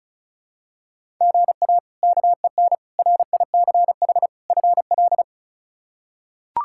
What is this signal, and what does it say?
Morse code, heard as a keyed tone, about 36 wpm, 700 Hz. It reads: GA KEN RICH FL
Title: K1USN SST QSO Practice - 35wpm